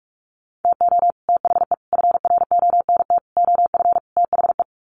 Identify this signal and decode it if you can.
Morse code, heard as a keyed tone, about 44 wpm, 700 Hz. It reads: TO THE FRONT OF THE
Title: Sets of 5 Words 45wpm